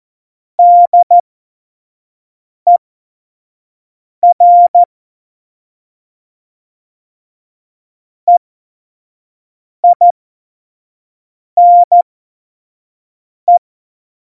Morse code, heard as a keyed tone, about 14 wpm, spaced as for 5 wpm, 700 Hz. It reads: DER EINE
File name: Morsekurs / Witze / Witz 31.Juli 2018, 25 F70